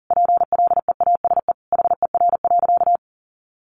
Morse code, heard as a keyed tone, about 40 wpm, 700 Hz. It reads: PLEASE HER.
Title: Sentences from Top 1500 Words 40wpm